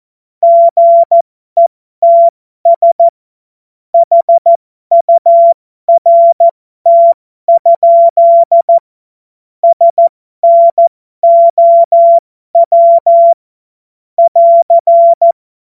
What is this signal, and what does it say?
Morse code, heard as a keyed tone, about 14 wpm, spaced as for 12 wpm, 680 Hz. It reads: GETS HURT? SNOW <AR>
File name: Morsekurs / Witze